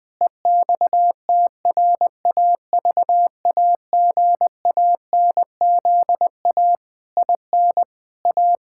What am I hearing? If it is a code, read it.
Morse code, heard as a keyed tone, about 20 wpm, 700 Hz. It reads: EXTRAVAGANZA IN A
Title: Pangram Challenge v2 - 20wpm